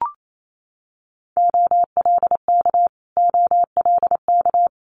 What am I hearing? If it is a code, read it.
Morse code, heard as a keyed tone, about 28 wpm, 700 Hz. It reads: OLK OLK
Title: Multi-letter Phonemes Repeated 3x 28wpm